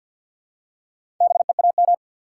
Morse code, heard as a keyed tone, about 50 wpm, 700 Hz. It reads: BEAM
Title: Top 180 QSO Elements 50wpm